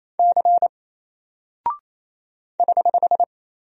Morse code, heard as a keyed tone, about 27 wpm, 700 Hz. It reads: C <HH>
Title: Single Character (Letter:Number:Punctuation:Prosign) Alt 28wpm